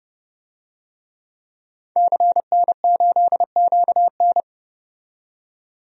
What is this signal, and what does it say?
Morse code, heard as a keyed tone, about 30 wpm, 700 Hz. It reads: CN8QN
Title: DXCC-100 Call Signs - 5 Characters - 30wpm